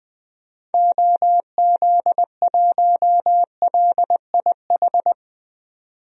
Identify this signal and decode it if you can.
Morse code, heard as a keyed tone, about 20 wpm, 700 Hz. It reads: OZ1LIH
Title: DXCC-100 Call Signs - 6 Characters - 20wpm